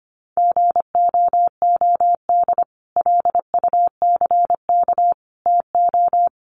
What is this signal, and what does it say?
Morse code, heard as a keyed tone, about 25 wpm, 700 Hz. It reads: GOOD LUCK TO
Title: Sentences from Top 1500 Words 25wpm